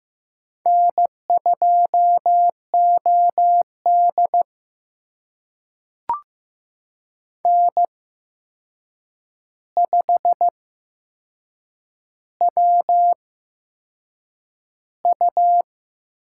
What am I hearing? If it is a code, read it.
Morse code, heard as a keyed tone, about 15 wpm, spaced as for 6 wpm, 700 Hz. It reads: N2OD N5WU
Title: USA Call Signs - 4 Characters - 8x Character Spacing - 15wpm